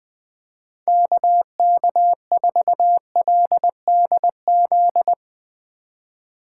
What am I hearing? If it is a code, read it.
Morse code, heard as a keyed tone, about 20 wpm, 700 Hz. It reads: KK4LDZ
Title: Speed Racing - USA 2x3 Call Signs 17wpm